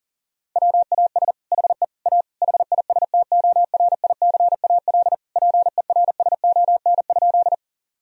Morse code, heard as a keyed tone, about 40 wpm, 700 Hz. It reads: WAS HE A HISTORICAL PERSON?
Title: Sentences from Top 1000 Words - Review Part 4 - 40wpm